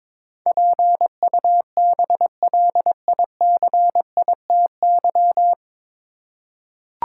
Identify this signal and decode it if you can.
Morse code, heard as a keyed tone, about 22 wpm, 700 Hz. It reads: PUBLICITY
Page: Morse Code Ninja practice 22wpm